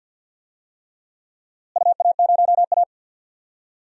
Morse code, heard as a keyed tone, about 50 wpm, 700 Hz. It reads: UA0A
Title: DXCC-100 Call Signs - 4 Characters - Repeated 3x - 50wpm